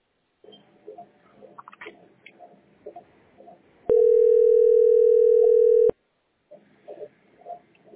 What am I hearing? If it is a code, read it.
Ringback tone (North America), 440 + 480 Hz.
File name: b31c88be-2799-4050-917f